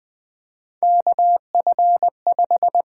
Morse code, heard as a keyed tone, about 20 wpm, 700 Hz. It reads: KF5
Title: USA Call Signs - 6 Characters - 8x Character Spacing - 20wpm